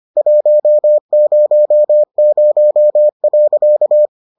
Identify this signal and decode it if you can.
Morse code, heard as a keyed tone, about 25 wpm, 600 Hz. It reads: 100.